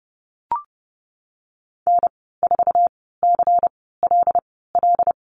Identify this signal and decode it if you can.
Morse code, heard as a keyed tone, about 30 wpm, spaced as for 17 wpm, 700 Hz. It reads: N4CLL
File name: course / 30:10wpm / 116 Lesson 19g - Letter w - Call Signs 30wpm